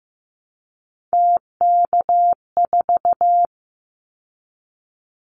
Morse code, heard as a keyed tone, about 15 wpm, 700 Hz. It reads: TK4
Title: Callsign Prefixes for Top 100 DXCC - 15wpm